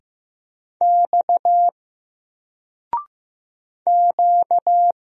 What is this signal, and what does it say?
Morse code, heard as a keyed tone, about 15 wpm, 700 Hz. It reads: X Q
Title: Single Letter-Number 15wpm